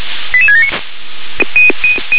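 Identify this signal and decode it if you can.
Five-tone selective call (ZVEI), heard as quick runs of five tones, tones of about 70 ms. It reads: 88679
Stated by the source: ZVEI five-tone selective call 88675 received with an SDR.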